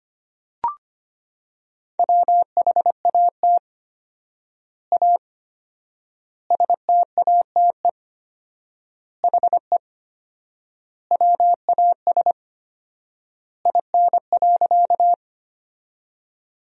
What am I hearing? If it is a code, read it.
Morse code, heard as a keyed tone, about 25 wpm, 700 Hz. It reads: WHAT A STATE HE WAS IN.
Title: 4x Word-Spacing - Sentences from Top 400 Words - 25wpm